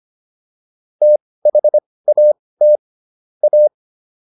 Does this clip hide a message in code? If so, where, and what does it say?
Yes: Morse code from 1.0 to 3.7 s, heard as a keyed tone, about 25 wpm, spaced as for 18 wpm, 600 Hz. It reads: THAT A